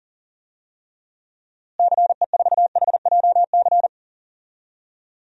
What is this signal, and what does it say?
Morse code, heard as a keyed tone, about 40 wpm, 700 Hz. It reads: CE4HJC
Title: DXCC-100 Call Signs - 6 Characters - Repeated 2x - 40wpm